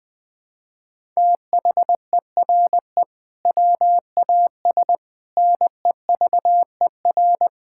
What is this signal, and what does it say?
Morse code, heard as a keyed tone, about 20 wpm, 700 Hz. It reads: THERE WAS NEVER
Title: Sentences from Top 1000 Words - Review Part 6 - 20wpm